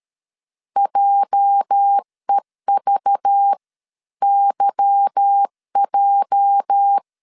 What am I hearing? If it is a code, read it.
Morse code, heard as a keyed tone, about 13 wpm, spaced as for 11 wpm, 800 Hz. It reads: JEVYJ
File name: LEZIONI / 8 YJ45 / 8 YJ 12wpm 2235